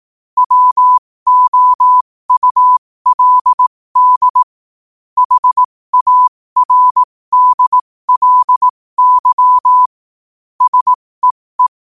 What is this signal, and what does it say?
Morse code, heard as a keyed tone, about 18 wpm, spaced as for 15 wpm, 1000 Hz. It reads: WOULD HARDLY SEE
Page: 15 WPM morse code quotes